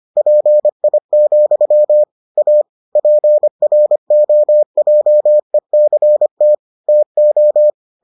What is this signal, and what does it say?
Morse code, heard as a keyed tone, about 25 wpm, 600 Hz. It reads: PI, A PROJECT TO